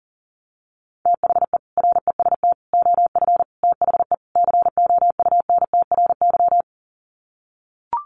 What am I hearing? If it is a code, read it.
Morse code, heard as a keyed tone, about 40 wpm, 700 Hz. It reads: THE REST OF THE COUNTRY
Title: Sets of 5 Words 40wpm